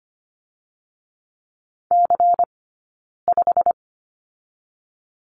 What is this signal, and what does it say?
Morse code, heard as a keyed tone, about 25 wpm, 700 Hz. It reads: C 5
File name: course / 25:10wpm / 092 Lesson 16d - Letter c - Two Characters 25wpm